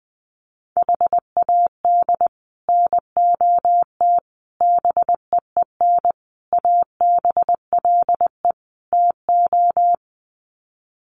Morse code, heard as a keyed tone, about 20 wpm, 700 Hz. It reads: HAD NOT BEEN ABLE TO
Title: Sets of 5 Words Encore 20wpm